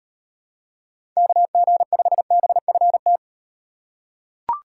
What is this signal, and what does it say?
Morse code, heard as a keyed tone, about 38 wpm, 700 Hz. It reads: KG5BFT